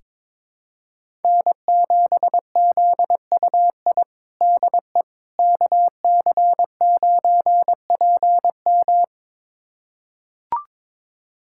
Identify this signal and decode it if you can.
Morse code, heard as a keyed tone, about 22 wpm, 700 Hz. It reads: N7ZUI DE KC9PM
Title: USA Callsign DE Callsign - 5 Characters - 22wpm